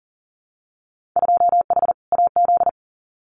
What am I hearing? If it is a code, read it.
Morse code, heard as a keyed tone, about 42 wpm, 700 Hz. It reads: 2H AZ
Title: Winter Field Day Exchanges 40wpm